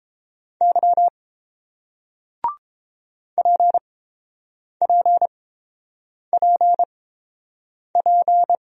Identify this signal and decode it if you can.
Morse code, heard as a keyed tone, about 29 wpm, spaced as for 7 wpm, 700 Hz. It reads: Y PPPP